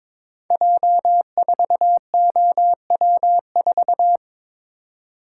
Morse code, heard as a keyed tone, about 22 wpm, 700 Hz. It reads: J4OW4
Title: Code Groups - 5 Letter:Numbers - 22wpm